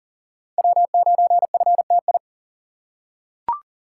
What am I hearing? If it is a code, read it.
Morse code, heard as a keyed tone, about 37 wpm, 700 Hz. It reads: W9FTI
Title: USA Call Signs - 5 Characters - 6x Character Spacing - 40wpm